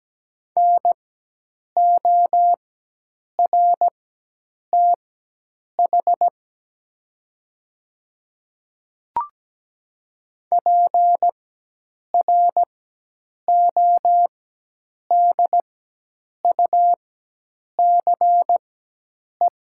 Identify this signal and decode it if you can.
Morse code, heard as a keyed tone, about 17 wpm, spaced as for 8 wpm, 700 Hz. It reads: NORTH PRODUCE
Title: 4x Character Spacing - Top 500 Words - 17wpm